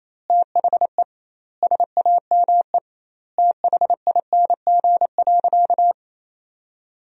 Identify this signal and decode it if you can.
Morse code, heard as a keyed tone, about 28 wpm, 700 Hz. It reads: THE SAME THING.
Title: 2x Word-Spacing - Sentences from Top 200 Words - 28wpm